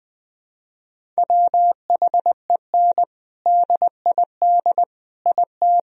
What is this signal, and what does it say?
Morse code, heard as a keyed tone, about 20 wpm, 700 Hz. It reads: WHEN DID IT
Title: Sentences from Top 1000 Words - Review Part 1 - 20wpm